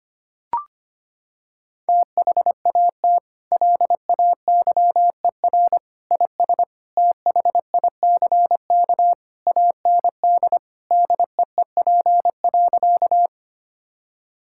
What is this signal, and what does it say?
Morse code, heard as a keyed tone, about 25 wpm, 700 Hz. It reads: THAT LAYER IS THICK AND DEEP.
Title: Sentences from Top 1600 Words 25wpm